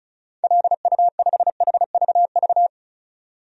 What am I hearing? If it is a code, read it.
Morse code, heard as a keyed tone, about 34 wpm, 700 Hz. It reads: LU5HVV
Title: DXCC-100 Call Signs - 6 Characters - 35wpm